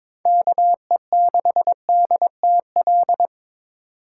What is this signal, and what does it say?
Morse code, heard as a keyed tone, about 21 wpm, 700 Hz. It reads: KE6DTL
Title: USA Call Signs - 6 Characters - 6x Character Spacing - 22wpm